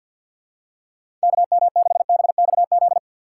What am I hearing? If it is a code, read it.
Morse code, heard as a keyed tone, about 51 wpm, 700 Hz. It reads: KM6BXZ
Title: USA Call Signs - 6 Characters - 4x Character Spacing - 50wpm